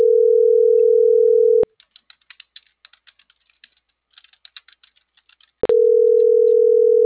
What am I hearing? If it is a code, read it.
Ringback tone (North America), 440 + 480 Hz.